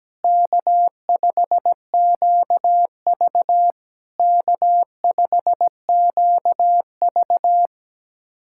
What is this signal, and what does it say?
Morse code, heard as a keyed tone, about 17 wpm, 700 Hz. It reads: K5QV K5QV
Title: SOTA QSO Practice 17wpm